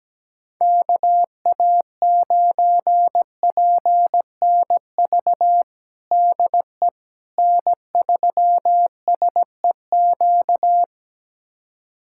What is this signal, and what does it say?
Morse code, heard as a keyed tone, about 17 wpm, 700 Hz. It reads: KA9PNV DE N3SEQ
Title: USA Callsign DE Callsign 17wpm